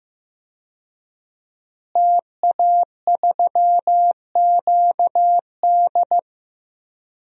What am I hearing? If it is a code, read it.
Morse code, heard as a keyed tone, about 15 wpm, 700 Hz. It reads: TA3QD
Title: DXCC-100 Call Signs - 5 Characters - Repeated 3x - 15wpm